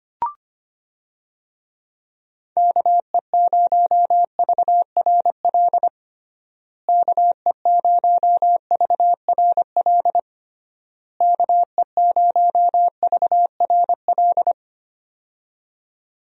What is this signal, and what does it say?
Morse code, heard as a keyed tone, about 25 wpm, 700 Hz. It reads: KE0VRL KE0VRL KE0VRL
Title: USA Call Signs - 6 Characters - Repeated 3x - 25wpm